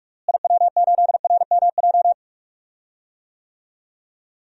Morse code, heard as a keyed tone, about 46 wpm, 700 Hz. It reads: IW8RMJ
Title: DXCC-100 Call Signs - 6 Characters - Repeated 2x - 45wpm